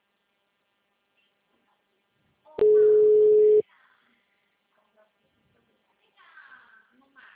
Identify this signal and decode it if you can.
Ringback tone (Europe, 425 Hz), with speech.
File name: out- -2016- - - .32266